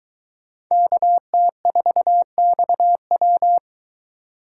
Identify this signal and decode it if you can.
Morse code, heard as a keyed tone, about 23 wpm, 700 Hz. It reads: KT4XW